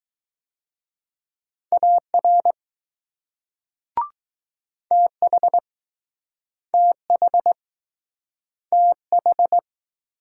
Morse code, heard as a keyed tone, about 20 wpm, 700 Hz. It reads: AR TH TH TH